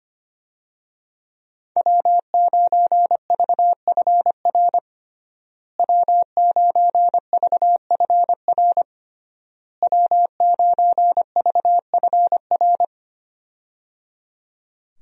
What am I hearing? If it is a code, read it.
Morse code, heard as a keyed tone, about 25 wpm, 700 Hz. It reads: W9VFR W9VFR W9VFR
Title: USA Call Signs - 5 Characters - Repeated 3x - 25wpm